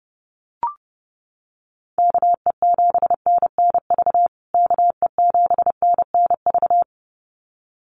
Morse code, heard as a keyed tone, about 30 wpm, 700 Hz. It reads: KE7NNV KE7NNV
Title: USA Call Signs Repeated 2x 30wpm